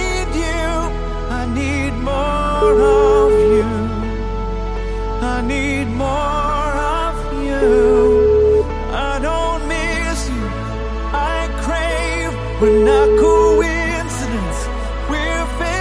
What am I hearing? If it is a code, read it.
Ringback tone (Europe, 425 Hz), with noise.